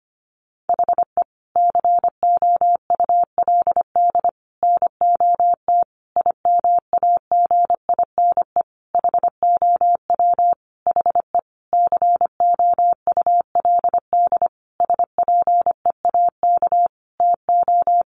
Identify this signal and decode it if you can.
Morse code, heard as a keyed tone, about 25 wpm, 700 Hz. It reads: HE COULD NOT IMAGINE HOW HE COULD SPEAK TO
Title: Sentences from Top 1400 Words 25wpm